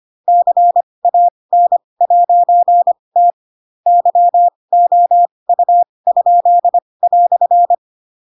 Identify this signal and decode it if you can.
Morse code, heard as a keyed tone, about 25 wpm, spaced as for 20 wpm, 700 Hz. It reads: CAN'T YOU?"
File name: Pinocchio / Pinocchio ch7 15 25 ik1hge